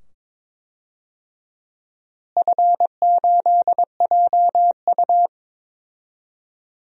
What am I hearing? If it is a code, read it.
Morse code, heard as a keyed tone, about 21 wpm, 700 Hz. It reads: F8JU
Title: DXCC-100 Call Signs - 4 Characters - Repeated 2x - 22wpm